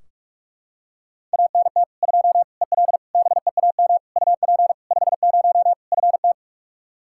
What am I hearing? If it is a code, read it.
Morse code, heard as a keyed tone, about 41 wpm, 700 Hz. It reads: ANT 2 EL BEAM UP 50 FT
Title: Practice QSO - Exchange 2 - Info - 45wpm